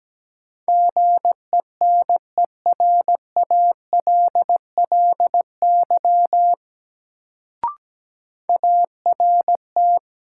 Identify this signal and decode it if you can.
Morse code, heard as a keyed tone, about 17 wpm, 700 Hz. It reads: GENERALLY ART
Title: Sentences from Top 600 Words - New Word Drill - 17wpm